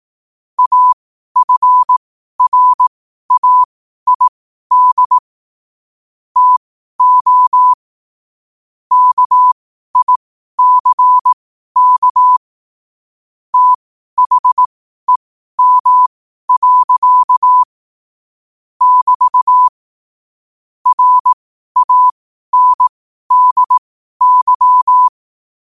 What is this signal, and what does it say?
Morse code, heard as a keyed tone, about 18 wpm, spaced as for 12 wpm, 1000 Hz. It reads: AFRAID TO KICK THEM. <BT> RANDY
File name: qotdfiles / 12wpm